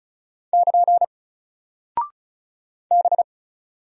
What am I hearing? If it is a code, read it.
Morse code, heard as a keyed tone, about 36 wpm, 700 Hz. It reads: <KN> B
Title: Single Character (Letter:Number:Punctuation:Prosign) Alt 35wpm